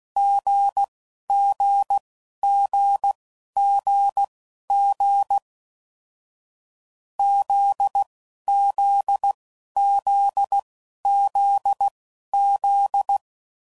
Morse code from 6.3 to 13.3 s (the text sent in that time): ZZZZZ